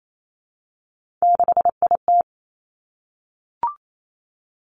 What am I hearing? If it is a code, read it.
Morse code, heard as a keyed tone, about 28 wpm, 700 Hz. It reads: 6IT
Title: Three Challenging Letter-Number Combinations 28wpm